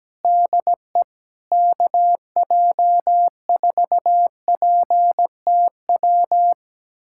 Morse code, heard as a keyed tone, about 17 wpm, 700 Hz. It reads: DE KJ4PTW
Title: USA Callsign DE Callsign - 6 Characters - 17wpm